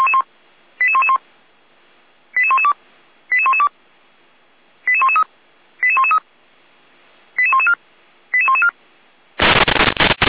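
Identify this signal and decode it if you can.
Five-tone selective call (ZVEI), heard as quick runs of five tones, tones of about 70 ms. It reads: 88111 88112 88112 88113 88113 88114 88114
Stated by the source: ZVEI five-tone call received by SDR, from the feuerwehr-orlamuende.org audio archive.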